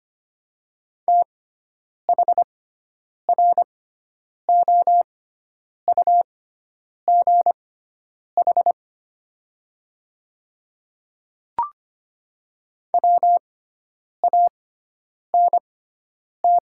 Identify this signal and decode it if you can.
Morse code, heard as a keyed tone, about 25 wpm, spaced as for 9 wpm, 700 Hz. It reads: THROUGH WANT
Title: 6x Character Spacing - Top 300 Words - 25wpm